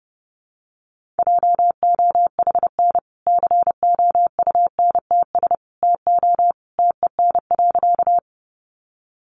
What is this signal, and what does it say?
Morse code, heard as a keyed tone, about 30 wpm, 700 Hz. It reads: JOHN COUNTS TO TEN.
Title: Easy Phrases 30wpm